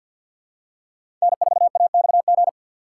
Morse code, heard as a keyed tone, about 53 wpm, 700 Hz. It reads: N4AXG
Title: USA Call Signs - 5 Characters - 3x Character Spacing - 50wpm